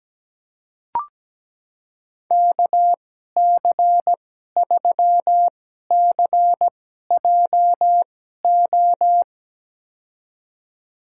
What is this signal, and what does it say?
Morse code, heard as a keyed tone, about 17 wpm, spaced as for 12 wpm, 700 Hz. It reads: KC3CJO
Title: USA Call Signs - 6 Characters - 2x Character Spacing - 17wpm